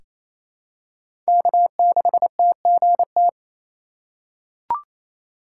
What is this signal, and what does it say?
Morse code, heard as a keyed tone, about 27 wpm, 700 Hz. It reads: K6TGT
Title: USA Call Signs - 5 Characters - 4x Character Spacing - 28wpm